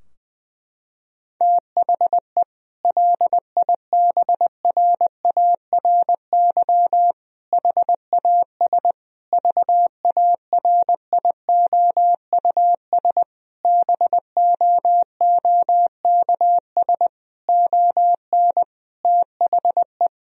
Morse code, heard as a keyed tone, about 20 wpm, 700 Hz. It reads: THE LIBRARY HAS VARIOUS BOOKS ON THE
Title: Sentences from Top 1000 Words - Review Part 4 - 20wpm